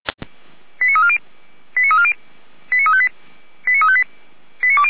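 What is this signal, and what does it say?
Five-tone selective call (ZVEI), heard as quick runs of five tones, tones of about 70 ms. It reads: 89340 89340 89358 89358
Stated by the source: ZVEI five-tone call received on an SDR.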